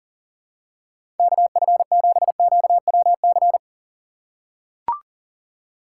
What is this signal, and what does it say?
Morse code, heard as a keyed tone, about 39 wpm, 700 Hz. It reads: KF7QWC
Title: USA Call Signs - 6 Characters - 3x Character Spacing - 40wpm